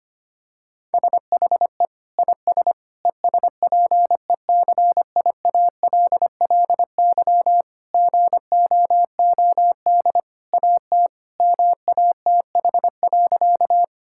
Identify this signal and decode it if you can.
Morse code, heard as a keyed tone, about 25 wpm, 700 Hz. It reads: SHE IS ESPECIALLY GOOD AT MATH.
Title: Sentences from Top 1000 Words - Review Part 6 - 25wpm